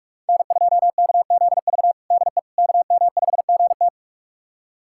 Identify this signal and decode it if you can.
Morse code, heard as a keyed tone, about 45 wpm, 700 Hz. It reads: N2KZV DE KM5GT
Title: USA Callsign DE Callsign - 5 Characters - 45wpm